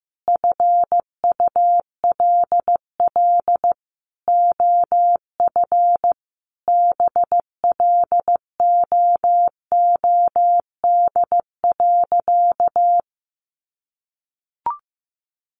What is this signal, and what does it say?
Morse code, heard as a keyed tone, about 15 wpm, 700 Hz. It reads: FULL OF BLOOD.
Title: Sentences from Top 1100 Words 15wpm